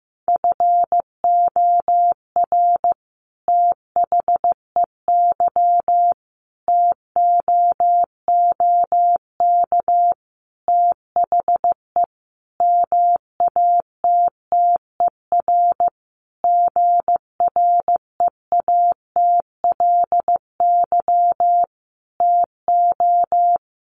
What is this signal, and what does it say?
Morse code, heard as a keyed tone, about 15 wpm, 700 Hz. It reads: FOR THEY TOOK THE MATTER GREATLY TO
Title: Sentences from Top 1400 Words 15wpm